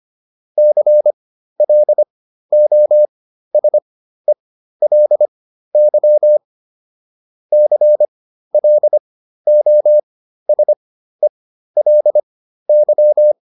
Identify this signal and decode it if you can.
Morse code, heard as a keyed tone, about 25 wpm, spaced as for 13 wpm, 600 Hz. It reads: CLOSELY CLOSELY